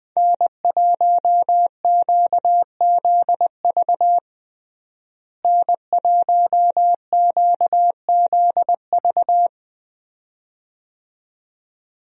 Morse code, heard as a keyed tone, about 20 wpm, 700 Hz. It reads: N1QZV N1QZV
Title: USA Call Signs - 5 Characters - Repeated 2x - 20wpm